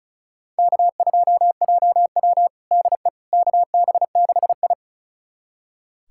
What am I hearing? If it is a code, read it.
Morse code, heard as a keyed tone, about 34 wpm, 700 Hz. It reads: K2JW DE KB6I
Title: USA Callsign DE Callsign - 4 Characters - 35wpm